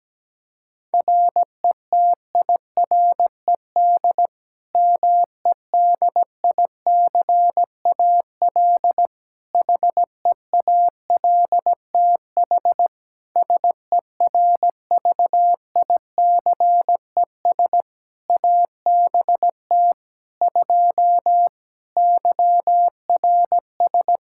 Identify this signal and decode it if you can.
Morse code, heard as a keyed tone, about 17 wpm, 700 Hz. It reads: RETIRED MEDICAL HEALTH SERVICES ABT 2 YRS
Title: Practice QSO - Exchange 3 - More Info - 17wpm